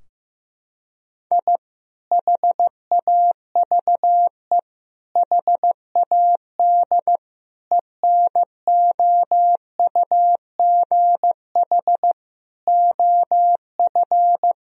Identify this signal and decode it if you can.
Morse code, heard as a keyed tone, about 15 wpm, 700 Hz. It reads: I HAVE HAD ENOUGH OF
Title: Sentences from Top 1000 Words - Review Part 3 - 15wpm